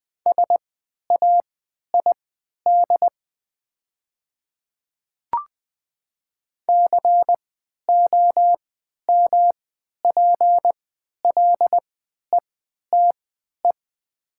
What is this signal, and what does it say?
Morse code, heard as a keyed tone, about 20 wpm, spaced as for 11 wpm, 700 Hz. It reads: SAID COMPLETE
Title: 3x Character Spacing - Top 500 Words - 20wpm